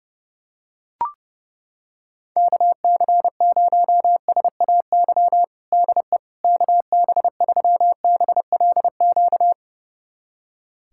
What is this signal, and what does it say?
Morse code, heard as a keyed tone, about 32 wpm, 700 Hz. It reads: KC0SAY DE KB3BLQ
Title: USA Callsign DE Callsign - 6 Characters - 30wpm